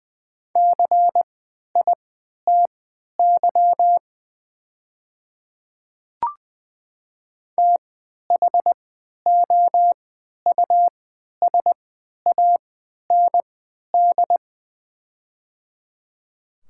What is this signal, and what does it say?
Morse code, heard as a keyed tone, about 20 wpm, spaced as for 11 wpm, 700 Hz. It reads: CITY THOUSAND
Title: 3x Character Spacing - Top 500 Words - 20wpm